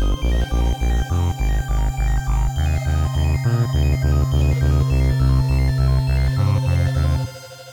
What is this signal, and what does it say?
There is no code.